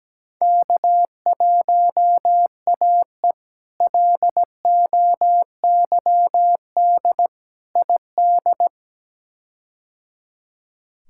Morse code, heard as a keyed tone, about 17 wpm, 700 Hz. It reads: K1AE LOYD ID
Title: K1USN SST QSO Practice - 17wpm